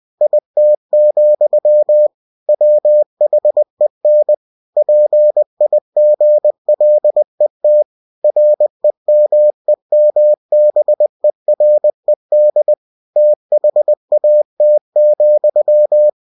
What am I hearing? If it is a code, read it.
Morse code, heard as a keyed tone, about 20 wpm, 600 Hz. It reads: IT, WHEN PIGLET REMEMBERED THAT,
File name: cw-books / winnie-the-pooh / winnie-20wpm-page-0037